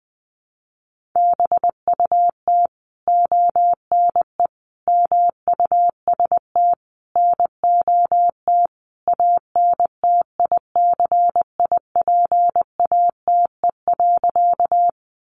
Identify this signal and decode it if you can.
Morse code, heard as a keyed tone, about 20 wpm, 700 Hz. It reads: BUT ONE MUST NOT ANTICIPATE.
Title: Sentences from Top 1500 Words 20wpm